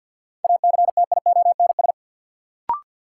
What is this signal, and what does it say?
Morse code, heard as a keyed tone, about 48 wpm, 700 Hz. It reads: ACTIONS
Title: Sentences from Top 1000 Words - Review Part 6 - New Word Drill - 50wpm